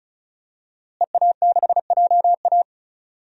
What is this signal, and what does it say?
Morse code, heard as a keyed tone, about 36 wpm, 700 Hz. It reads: EA6JA
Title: DXCC-100 Call Signs - 5 Characters - Repeated 2x - 35wpm